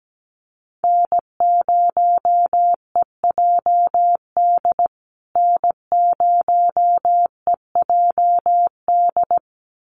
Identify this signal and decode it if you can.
Morse code, heard as a keyed tone, about 17 wpm, 700 Hz. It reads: N0EJD N0EJD
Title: USA Call Signs Repeated 2x 17wpm